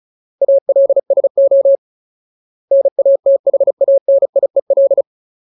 Morse code, heard as a keyed tone, about 36 wpm, 550 Hz. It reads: ALSO NATHANIEL
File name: Ultimate Word List / 35WPM Ultimate